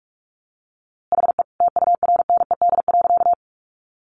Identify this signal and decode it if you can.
Morse code, heard as a keyed tone, about 44 wpm, 700 Hz. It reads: HE TURNED.
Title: A Sound Of Thunder 45wpm